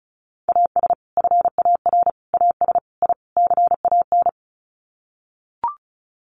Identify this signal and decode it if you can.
Morse code, heard as a keyed tone, about 34 wpm, 700 Hz. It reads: AS FAR AS I CAN
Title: Sets of 5 Words Encore 35wpm